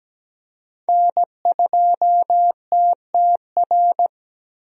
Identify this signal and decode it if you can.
Morse code, heard as a keyed tone, about 17 wpm, 700 Hz. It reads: N2TTR
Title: USA Call Signs - 5 Characters - Repeated 2x - 17wpm